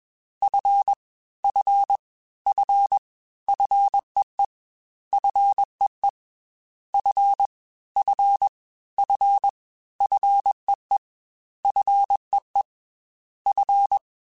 Morse code, heard as a keyed tone, about 21 wpm, 800 Hz. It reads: F F F FEE FEE F F F FEE FEE F
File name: F and L / F and L mp3 / F and L Prefixes 3x2 Twice @20